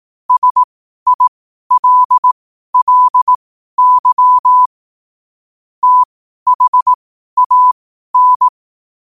Morse code, heard as a keyed tone, about 18 wpm, spaced as for 12 wpm, 1000 Hz. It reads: SILLY THAN